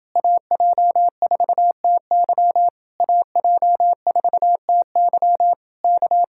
Morse code, heard as a keyed tone, about 28 wpm, 700 Hz. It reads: AJ4TY AJ4TY K